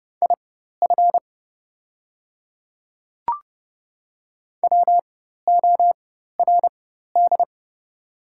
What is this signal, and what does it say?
Morse code, heard as a keyed tone, about 32 wpm, spaced as for 14 wpm, 690 Hz. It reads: IF WORD
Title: 4x Character Spacing - Top 200 Words - 30wpm